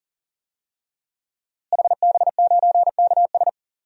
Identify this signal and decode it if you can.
Morse code, heard as a keyed tone, about 40 wpm, 700 Hz. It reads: HB9KS
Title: DXCC-100 Call Signs - 5 Characters - Repeated 2x - 40wpm